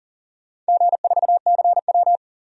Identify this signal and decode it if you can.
Morse code, heard as a keyed tone, about 40 wpm, 700 Hz. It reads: G4CW
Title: DXCC-100 Call Signs - 4 Characters - 40wpm